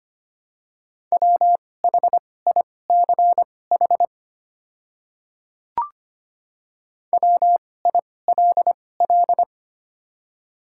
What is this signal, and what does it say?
Morse code, heard as a keyed tone, about 25 wpm, spaced as for 18 wpm, 700 Hz. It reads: WHICH WILL
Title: 2x Character Spacing - Top 100 Words - 25wpm